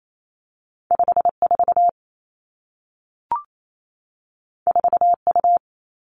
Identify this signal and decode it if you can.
Morse code, heard as a keyed tone, about 28 wpm, 700 Hz. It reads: 5 4 4 U
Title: Two Challenging Letter-Number Combinations 28wpm